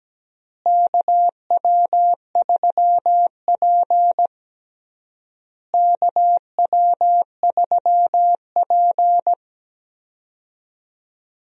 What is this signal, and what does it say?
Morse code, heard as a keyed tone, about 17 wpm, 700 Hz. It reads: KW3P KW3P
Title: USA Call Signs - 4 Characters - Repeated 3x - 17wpm